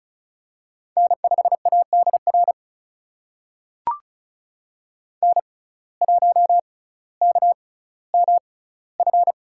Morse code, heard as a keyed tone, about 34 wpm, spaced as for 18 wpm, 700 Hz. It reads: N5ADR N1KMF
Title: USA Call Signs - 5 Characters - 6x Character Spacing - 35wpm